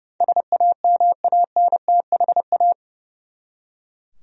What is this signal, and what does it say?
Morse code, heard as a keyed tone, about 30 wpm, 700 Hz. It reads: SAMANTHA
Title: USA Common Names - 30wpm